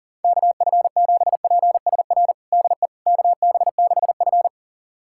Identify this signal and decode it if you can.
Morse code, heard as a keyed tone, about 40 wpm, 700 Hz. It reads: KF7PSR DE KB6F
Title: USA Callsign DE Callsign 40wpm